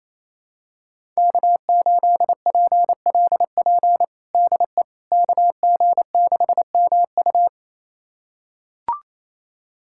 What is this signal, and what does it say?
Morse code, heard as a keyed tone, about 28 wpm, 700 Hz. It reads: K8PLP DE KG6MU
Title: USA Callsign DE Callsign - 5 Characters - 28wpm